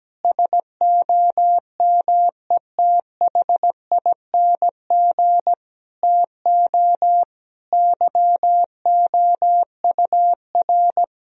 Morse code, heard as a keyed tone, about 17 wpm, 700 Hz. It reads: SOMETHING TO YOUR
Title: Sentences from Top 1000 Words - Review Part 3 - 17wpm